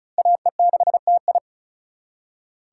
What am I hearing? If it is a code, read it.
Morse code, heard as a keyed tone, about 47 wpm, spaced as for 41 wpm, 700 Hz. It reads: AE6TI
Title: Speed Racing - USA 2x2 Call Signs 35wpm